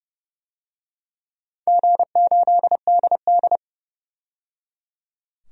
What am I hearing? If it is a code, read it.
Morse code, heard as a keyed tone, about 30 wpm, 700 Hz. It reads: G8DD
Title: DXCC-100 Call Signs - 4 Characters - 30wpm